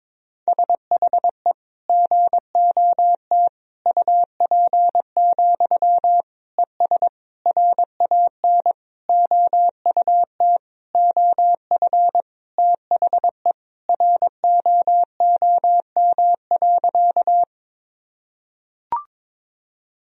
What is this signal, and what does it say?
Morse code, heard as a keyed tone, about 22 wpm, 700 Hz. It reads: SHE GOT UP, ES RAN OUT OF THE ROOM.
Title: Sentences from Top 500 Words and Abbreviations 22wpm